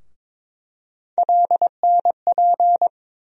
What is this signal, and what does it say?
Morse code, heard as a keyed tone, about 22 wpm, 700 Hz. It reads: LNP
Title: USA Call Sign Postfixes - 3 Letters 22wpm